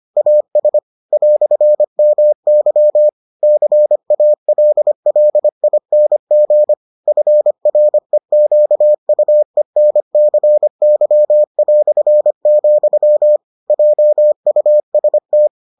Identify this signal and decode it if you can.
Morse code, heard as a keyed tone, about 25 wpm, 600 Hz. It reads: AS "MY CALLING FREQUENCY", JUST